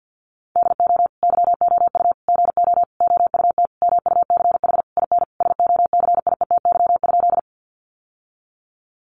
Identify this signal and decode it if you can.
Morse code, heard as a keyed tone, about 50 wpm, 700 Hz. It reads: DO YOU GO OUT MUCH IN SOCIETY?
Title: Sentences from Top 700 Words 50wpm